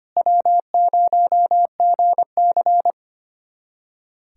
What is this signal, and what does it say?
Morse code, heard as a keyed tone, about 25 wpm, 700 Hz. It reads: W0GC
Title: USA Callsign DE Callsign - 4 Characters - 25wpm